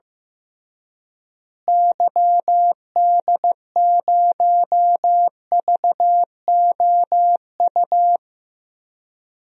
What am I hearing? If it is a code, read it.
Morse code, heard as a keyed tone, about 15 wpm, 700 Hz. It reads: YD0VOU
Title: DXCC-100 Call Signs - 6 Characters - Repeated 3x - 15wpm